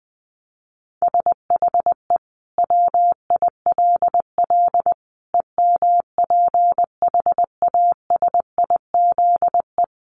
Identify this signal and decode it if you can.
Morse code, heard as a keyed tone, about 19 wpm, 700 Hz. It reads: SHE WILL EMPHASIZE
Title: Sentences from Top 1500 Words 20wpm